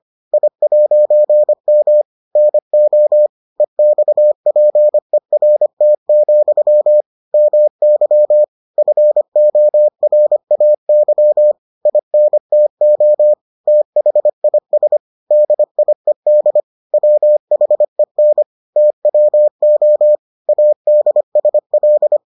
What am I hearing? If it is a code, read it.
Morse code, heard as a keyed tone, about 25 wpm, 600 Hz. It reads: I'M NO EXPERT, MY FORAY INTO THIS DIED WHEN TWO ADSL